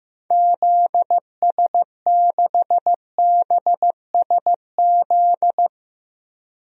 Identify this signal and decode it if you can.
Morse code, heard as a keyed tone, about 15 wpm, 700 Hz. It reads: ZS6BSZ
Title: DXCC-100 Call Signs - 6 Characters - 15wpm